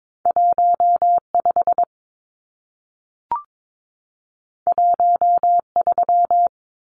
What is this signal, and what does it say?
Morse code, heard as a keyed tone, about 22 wpm, 700 Hz. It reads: 15 13
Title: Numbers - 2-Digit 22wpm